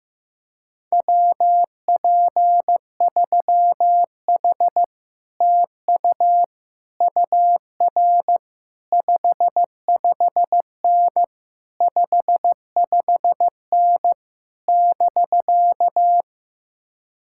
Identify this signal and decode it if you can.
Morse code, heard as a keyed tone, about 15 wpm, 700 Hz. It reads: WP3H TU UR 55N 55N <BK>
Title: Speed-Racing Common QSO Phrases Encore - 15wpm